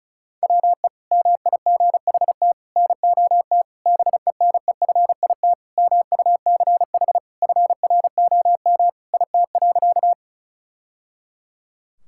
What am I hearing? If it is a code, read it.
Morse code, heard as a keyed tone, about 34 wpm, 700 Hz. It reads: WE MIGHT NOT BENEFIT MUCH FROM IT.
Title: Sentences from Top 1000 Words - Review Part 6 - 35wpm